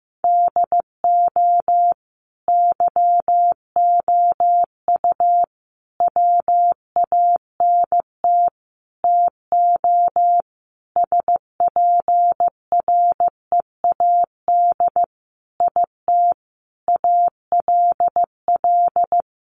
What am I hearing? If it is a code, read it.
Morse code, heard as a keyed tone, about 15 wpm, 700 Hz. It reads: DO YOU WANT TO SPREAD IT ALL
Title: Sentences from Top 1200 Words 15wpm